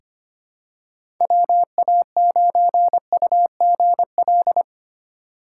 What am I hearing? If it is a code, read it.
Morse code, heard as a keyed tone, about 25 wpm, 700 Hz. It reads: WA9UGL
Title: USA Call Signs - 6 Characters - 2x Character Spacing - 25wpm